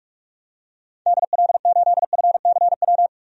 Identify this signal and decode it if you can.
Morse code, heard as a keyed tone, about 46 wpm, 700 Hz. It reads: DL8FCW
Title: DXCC-100 Call Signs - 6 Characters - 45wpm